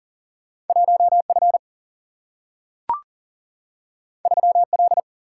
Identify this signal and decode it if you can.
Morse code, heard as a keyed tone, about 40 wpm, 700 Hz. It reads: 1 F 3 L
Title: Single-Letter Single-Number 40wpm